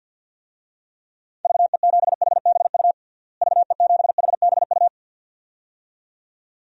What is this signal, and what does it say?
Morse code, heard as a keyed tone, about 53 wpm, 700 Hz. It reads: VE7HBU VE7HBU
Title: DXCC-100 Call Signs - 6 Characters - Repeated 2x - 50wpm